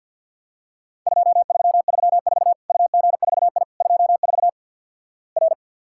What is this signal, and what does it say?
Morse code, heard as a keyed tone, about 50 wpm, 700 Hz. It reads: 2334 UG4I 24 R
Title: Worked All Europe DX Contest - QTC Exchange with Repeats - 50wpm